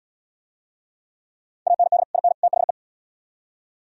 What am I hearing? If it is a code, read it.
Morse code, heard as a keyed tone, about 71 wpm, 700 Hz. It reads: AND IN THE